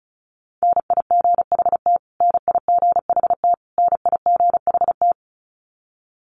Morse code, heard as a keyed tone, about 34 wpm, 700 Hz. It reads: NIGHT NIGHT NIGHT
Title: Top 500 Words Repeated 35wpm